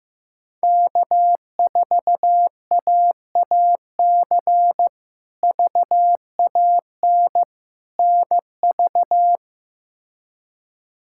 Morse code, heard as a keyed tone, about 15 wpm, 700 Hz. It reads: K4AAC VAN NV
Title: K1USN SST QSO Practice - 15wpm